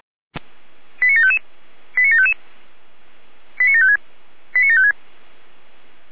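Five-tone selective call (ZVEI), heard as quick runs of five tones, tones of about 80 ms. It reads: 89755 89755 89756 89756
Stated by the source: ZVEI five-tone call 89465 received by SDR.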